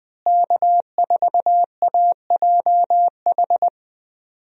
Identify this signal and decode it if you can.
Morse code, heard as a keyed tone, about 20 wpm, 700 Hz. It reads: K4AJH
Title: USA Call Signs - 5 Characters - Repeated 2x - 20wpm